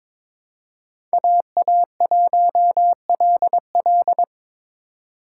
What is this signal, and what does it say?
Morse code, heard as a keyed tone, about 22 wpm, 700 Hz. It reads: AA1LL
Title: USA Call Signs - 5 Characters - 3x Character Spacing - 22wpm